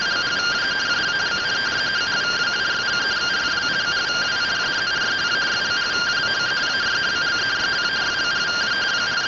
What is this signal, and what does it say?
Two-tone teleprinter signal, about 170 Hz shift.